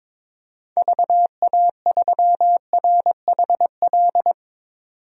Morse code, heard as a keyed tone, about 22 wpm, 700 Hz. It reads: VA3RHL
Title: DXCC-100 Call Signs - 6 Characters - 22wpm